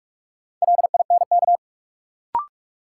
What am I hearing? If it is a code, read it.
Morse code, heard as a keyed tone, about 42 wpm, 700 Hz. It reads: LINK
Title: Sentences from Top 1000 Words - Review Part 7 - New Word Drill - 45wpm